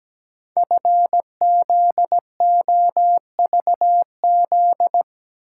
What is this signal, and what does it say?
Morse code, heard as a keyed tone, about 17 wpm, 700 Hz. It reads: FZOVZ
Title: Code Groups - 5 Letters - 17wpm